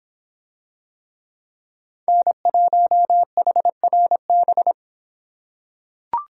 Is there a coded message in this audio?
Yes: Morse code from 2.1 to 4.7 s, heard as a keyed tone, about 24 wpm, 700 Hz. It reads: N1HRB